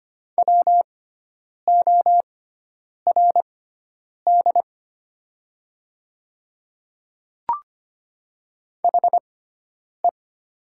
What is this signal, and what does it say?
Morse code, heard as a keyed tone, about 25 wpm, spaced as for 9 wpm, 700 Hz. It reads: WORD HE
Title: 6x Character Spacing - Top 100 Words - 25wpm